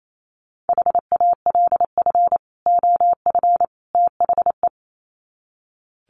Morse code, heard as a keyed tone, about 28 wpm, 700 Hz. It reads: HALF OF THE
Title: Sets of 3 Words 28wpm